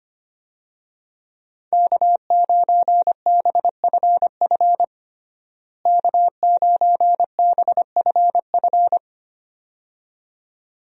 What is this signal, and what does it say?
Morse code, heard as a keyed tone, about 25 wpm, 700 Hz. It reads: K9BFF K9BFF
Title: USA Call Signs - 5 Characters - Repeated 2x - 25wpm